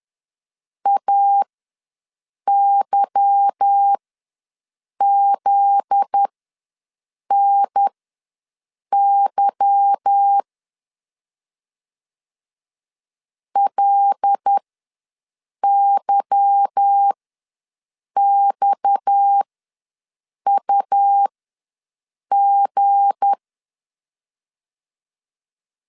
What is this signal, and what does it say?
Morse code, heard as a keyed tone, about 10 wpm, spaced as for 6 wpm, 800 Hz. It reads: AYZNY LYXUG
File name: LEZIONI / 8 YJ45 / 8 Y ws 10wpm 3952